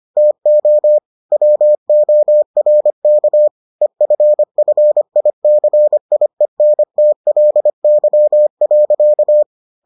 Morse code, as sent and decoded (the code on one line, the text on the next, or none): - --- / .-- --- .-. -.- / . ..-. ..-. .. -.-. .. -. - .-.. -.-- .-.-.-
TO WORK EFFICIENTLY.